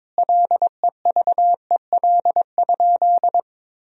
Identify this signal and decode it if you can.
Morse code, heard as a keyed tone, about 22 wpm, 700 Hz. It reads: LEVEL?
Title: Sentences from Top 1000 Words - Review Part 4 - 22wpm